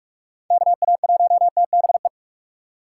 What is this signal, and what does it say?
Morse code, heard as a keyed tone, about 46 wpm, 700 Hz. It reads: KA1TBE
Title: USA Call Signs - 6 Characters - 6x Character Spacing - 45wpm